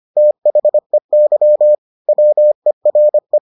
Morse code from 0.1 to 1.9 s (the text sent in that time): THEY